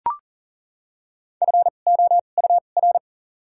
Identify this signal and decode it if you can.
Morse code, heard as a keyed tone, about 40 wpm, 700 Hz. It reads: F O U R
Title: 2x Character Spacing - Top 300 Words - 40wpm